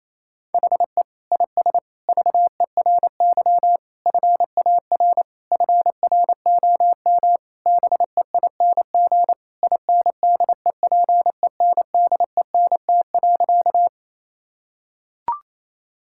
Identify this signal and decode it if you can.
Morse code, heard as a keyed tone, about 28 wpm, 700 Hz. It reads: HE IS VERY FAR FROM BEING INDEPENDENT.
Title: Sentences from Top 1000 Words - Review Part 6 - 28wpm